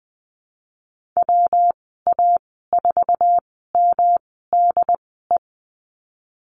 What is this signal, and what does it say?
Morse code, heard as a keyed tone, about 19 wpm, spaced as for 14 wpm, 700 Hz. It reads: WA4MDE
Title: 140 Lesson 23g - Letter f - Call Signs 20wpm